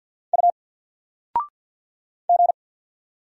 Morse code, heard as a keyed tone, about 50 wpm, 700 Hz. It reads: U G
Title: Single Letter-Number 50wpm